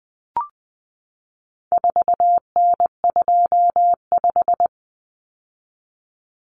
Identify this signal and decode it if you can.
Morse code, heard as a keyed tone, about 20 wpm, 700 Hz. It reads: 4N25
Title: Common Cut Numbers - 4-Digit 20wpm